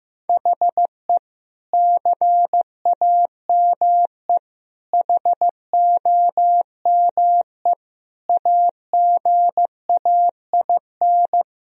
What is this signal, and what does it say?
Morse code, heard as a keyed tone, about 15 wpm, 700 Hz. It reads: HE CAME HOME AGAIN
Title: Sentences from Top 1000 Words - Review Part 2 - 15wpm